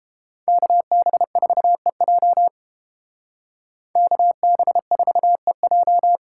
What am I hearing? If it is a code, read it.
Morse code, heard as a keyed tone, about 31 wpm, 700 Hz. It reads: KB4EJ KB4EJ